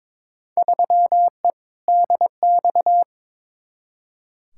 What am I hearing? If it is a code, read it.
Morse code, heard as a keyed tone, about 22 wpm, 700 Hz. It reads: 3E DX
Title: Field Day Exchanges Encore - 22wpm